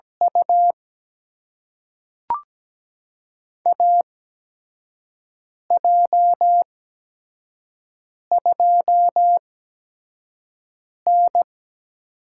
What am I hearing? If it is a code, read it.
Morse code, heard as a keyed tone, about 17 wpm, spaced as for 5 wpm, 700 Hz. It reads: U AJ2N
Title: USA Call Signs - 8x Character Spacing - 17wpm